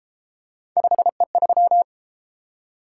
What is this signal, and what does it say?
Morse code, heard as a keyed tone, about 31 wpm, 700 Hz. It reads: 5E3